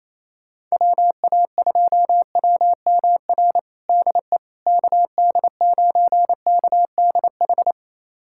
Morse code, heard as a keyed tone, about 28 wpm, 700 Hz. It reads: WA2WMR DE KD9KDH
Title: USA Callsign DE Callsign - 6 Characters - 28wpm